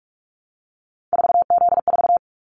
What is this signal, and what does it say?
Morse code, heard as a keyed tone, about 44 wpm, 700 Hz. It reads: 4Z4
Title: Callsign Prefixes for Top 50 DXCC - 45wpm